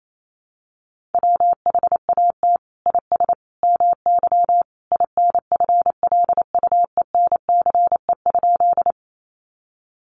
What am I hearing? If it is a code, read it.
Morse code, heard as a keyed tone, about 28 wpm, 700 Hz. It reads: WHAT IS MY INFLUENCE?
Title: Sentences from Top 1100 Words 28wpm